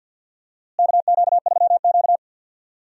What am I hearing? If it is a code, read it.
Morse code, heard as a keyed tone, about 46 wpm, 700 Hz. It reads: KQ3Q
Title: USA Call Signs - 4 Characters - Repeated 3x - 50wpm